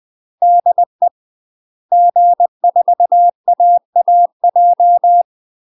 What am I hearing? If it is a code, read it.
Morse code, heard as a keyed tone, about 20 wpm, 700 Hz. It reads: DE G4AAJ
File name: UK MASTER QSOs TRAINING / 20WPM / UK MASTER QSOs 0074